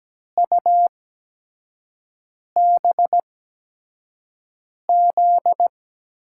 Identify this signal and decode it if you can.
Morse code, heard as a keyed tone, about 17 wpm, spaced as for 5 wpm, 700 Hz. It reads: UBZ
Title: USA Call Signs - 6 Characters - 8x Character Spacing - 17wpm